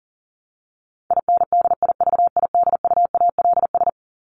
Morse code, heard as a keyed tone, about 40 wpm, 700 Hz. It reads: INDIVIDUALS
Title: ES:S Suffix - Top 100 Words with Base Words 40wpm